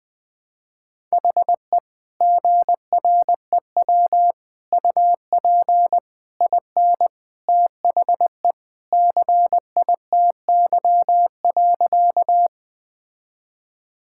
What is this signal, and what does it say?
Morse code, heard as a keyed tone, about 20 wpm, 700 Hz. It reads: HE GREW UP IN THE CITY.
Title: Sentences with 3-Word Phrases - Spoken Beginning - 20wpm